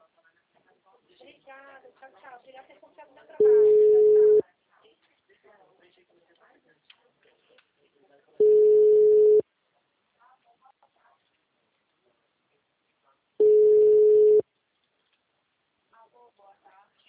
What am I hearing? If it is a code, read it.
Ringback tone (Europe, 425 Hz), with speech.